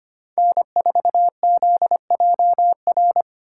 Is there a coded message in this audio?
Yes: Morse code from 0.4 to 3.2 s, heard as a keyed tone, about 33 wpm, 700 Hz. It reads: N4ZJR